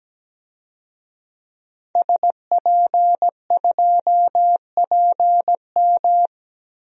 Morse code, heard as a keyed tone, about 17 wpm, 700 Hz. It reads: SP2PM
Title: DXCC-100 Call Signs - 5 Characters - Repeated 3x - 17wpm